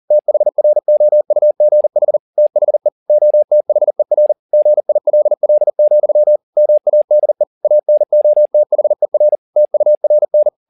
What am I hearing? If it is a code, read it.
Morse code, heard as a keyed tone, about 40 wpm, 600 Hz. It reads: THROUGH THE OTHER GILL, MADE ANOTHER TURN
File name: the old man and the sea 2 40wpm